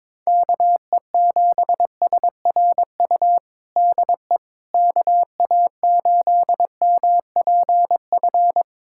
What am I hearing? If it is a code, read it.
Morse code, heard as a keyed tone, about 22 wpm, 700 Hz. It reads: KE7SRU DE KA8MPF
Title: USA Callsign DE Callsign 22wpm